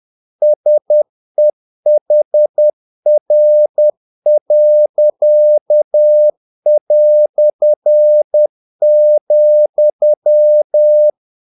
Morse code, heard as a keyed tone, about 10 wpm, 600 Hz. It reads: SEHR.",
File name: dk5ke / Humor-50BpM